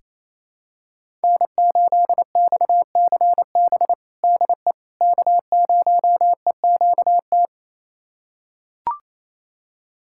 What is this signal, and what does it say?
Morse code, heard as a keyed tone, about 27 wpm, 700 Hz. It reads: N8XCB DE K0EQT
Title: USA Callsign DE Callsign - 5 Characters - 28wpm